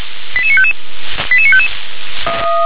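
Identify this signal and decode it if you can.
Five-tone selective call (ZVEI), heard as quick runs of five tones, tones of about 70 ms. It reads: 88955 88955
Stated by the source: ZVEI five-tone selective call received with an SDR.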